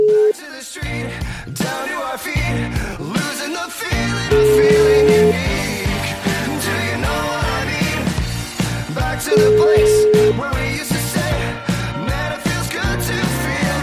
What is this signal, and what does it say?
Ringback tone (Europe, 425 Hz), with noise.